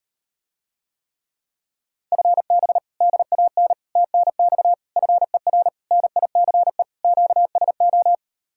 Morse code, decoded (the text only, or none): FB DAN TNX FER NICE QSO